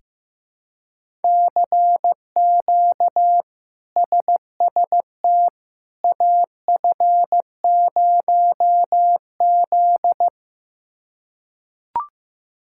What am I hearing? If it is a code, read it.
Morse code, heard as a keyed tone, about 15 wpm, 700 Hz. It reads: CQ SST AF0Z
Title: K1USN SST QSO Practice - 15wpm